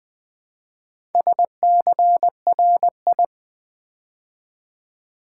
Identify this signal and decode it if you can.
Morse code, heard as a keyed tone, about 20 wpm, 700 Hz. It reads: SCRI
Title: Build A Word - 2 Letters At A Time - 6-Letter Words - 20wpm